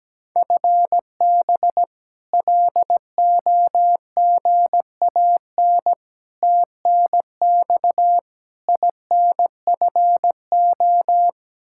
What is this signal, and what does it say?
Morse code, heard as a keyed tone, about 17 wpm, 700 Hz. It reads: FB LOGAN TNX INFO
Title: K1USN SST QSO Practice 17wpm